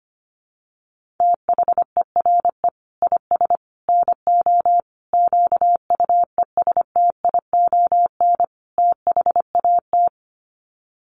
Morse code, heard as a keyed tone, about 25 wpm, 700 Hz. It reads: THERE IS NO QUESTION THAT
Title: Sets of 5 Words Encore 25wpm